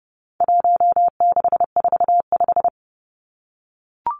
Morse code, heard as a keyed tone, about 30 wpm, 700 Hz. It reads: 1645
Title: Numbers - 4-Digit 30wpm